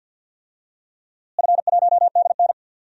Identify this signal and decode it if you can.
Morse code, heard as a keyed tone, about 52 wpm, 700 Hz. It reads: F1DN
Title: DXCC-100 Call Signs - 4 Characters - Repeated 3x - 50wpm